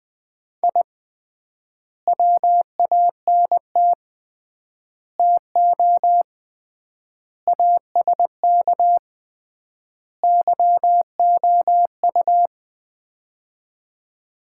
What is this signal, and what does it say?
Morse code, heard as a keyed tone, about 20 wpm, 690 Hz. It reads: I WANT TO ASK YOU
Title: 3x Word-Spacing - Sets of 5 Words - 20wpm